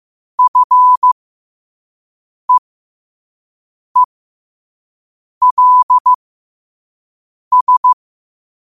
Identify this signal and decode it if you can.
Morse code, heard as a keyed tone, about 15 wpm, spaced as for 5 wpm, 1000 Hz. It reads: FEELS
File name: qotdfiles / 5wpm